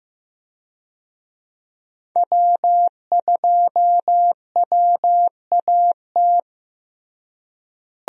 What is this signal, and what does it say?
Morse code, heard as a keyed tone, about 16 wpm, 700 Hz. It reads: W2WAT
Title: USA Call Signs - 5 Characters - Repeated 3x - 15wpm